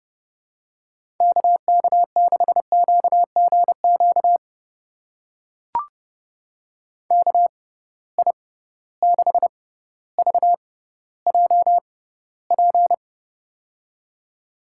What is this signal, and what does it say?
Morse code, heard as a keyed tone, about 31 wpm, spaced as for 16 wpm, 700 Hz. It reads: KK6QGQ KI6VJP
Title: USA Call Signs - 6 Characters - 6x Character Spacing - 30wpm